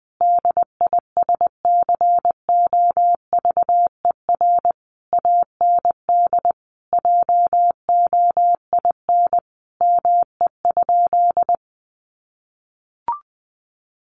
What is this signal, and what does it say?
Morse code, heard as a keyed tone, about 20 wpm, 700 Hz. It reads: DISCOVER AND JOIN ME?
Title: Sentences from Top 1400 Words 20wpm